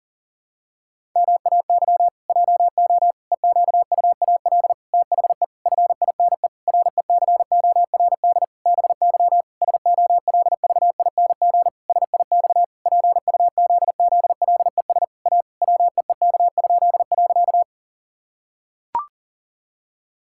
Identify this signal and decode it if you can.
Morse code, heard as a keyed tone, about 40 wpm, 700 Hz. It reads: MAY JO EQUAL THE FINE RECORD BY SOLVING SIX PUZZLES A WEEK?.
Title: Pangram Challenge v2 - 40wpm